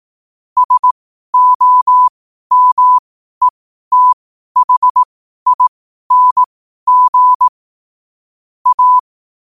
Morse code, heard as a keyed tone, about 18 wpm, spaced as for 12 wpm, 1000 Hz. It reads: SOMETHING A
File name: qotdfiles / 12wpm